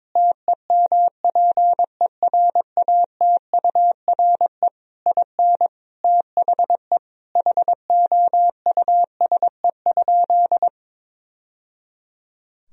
Morse code, heard as a keyed tone, about 22 wpm, 700 Hz. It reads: TEMPERATURE IN THE HOUSE?
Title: Sentences from Top 1000 Words - Review Part 4 - 22wpm